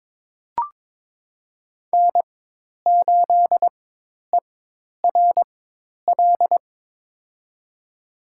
Morse code, heard as a keyed tone, about 21 wpm, spaced as for 10 wpm, 700 Hz. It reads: N8ERL
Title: USA Call Signs - 5 Characters - 4x Character Spacing - 22wpm